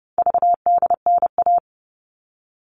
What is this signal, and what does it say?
Morse code, heard as a keyed tone, about 30 wpm, 700 Hz. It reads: VDNA
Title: All Cut Numbers - 4-Digit 30wpm